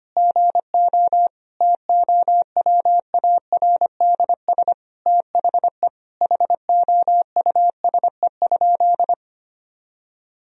Morse code, heard as a keyed tone, about 25 wpm, 700 Hz. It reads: GO TOWARDS THE HOUSE?
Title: Sentences from Top 1000 Words - Review Part 2 - 25wpm